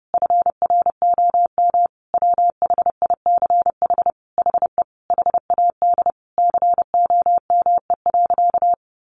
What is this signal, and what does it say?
Morse code, heard as a keyed tone, about 30 wpm, 700 Hz. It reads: FROM WHICH HE HAD COME.
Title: Sentences from Top 1500 Words 30wpm